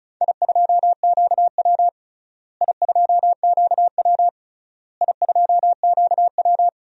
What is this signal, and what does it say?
Morse code, heard as a keyed tone, about 36 wpm, 700 Hz. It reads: I2QW I2QW I2QW
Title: DXCC-100 Call Signs - 4 Characters - Repeated 3x - 35wpm